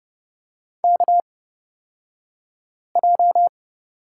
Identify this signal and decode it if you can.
Morse code, heard as a keyed tone, about 30 wpm, spaced as for 5 wpm, 700 Hz. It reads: KJ
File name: Lesson 37d - Letter x - Two Characters 30wpm